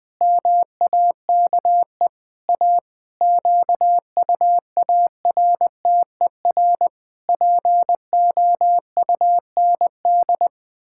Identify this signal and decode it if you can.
Morse code, heard as a keyed tone, about 21 wpm, 700 Hz. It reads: MAKE A QUARTER POUND
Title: Pangram Challenge v2 - 20wpm